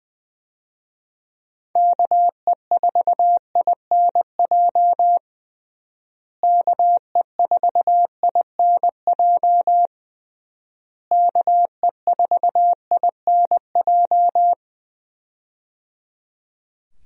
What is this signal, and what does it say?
Morse code, heard as a keyed tone, about 20 wpm, 700 Hz. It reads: KE4INJ KE4INJ KE4INJ
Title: USA Call Signs - 6 Characters - Repeated 3x - 20wpm